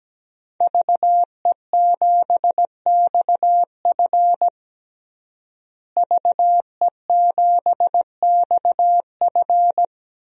Morse code, heard as a keyed tone, about 17 wpm, 700 Hz. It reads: VE7XF VE7XF
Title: DXCC-100 Call Signs - 5 Characters - Repeated 2x - 17wpm